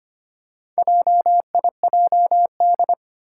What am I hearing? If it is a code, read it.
Morse code, heard as a keyed tone, about 25 wpm, 700 Hz. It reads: JIJD
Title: Four Challenging Letter-Number Combinations 25wpm